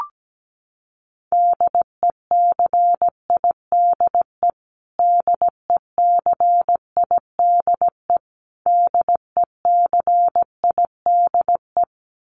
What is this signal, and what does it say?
Morse code, heard as a keyed tone, about 17 wpm, 700 Hz. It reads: DECIDE DECIDE DECIDE
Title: Top 500 Words Repeated 17wpm